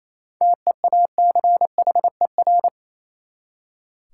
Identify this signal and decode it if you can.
Morse code, heard as a keyed tone, about 28 wpm, 700 Hz. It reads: TEACHER
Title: Seven Letter Words - 28wpm